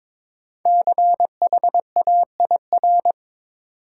Morse code, heard as a keyed tone, about 22 wpm, 700 Hz. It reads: CHAIR
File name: 5 Letter Words - First 3 Letters Given - 22wpm